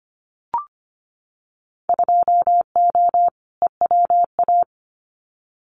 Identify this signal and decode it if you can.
Morse code, heard as a keyed tone, about 25 wpm, 700 Hz. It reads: 2O EWA
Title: Winter Field Day Exchanges Encore 25wpm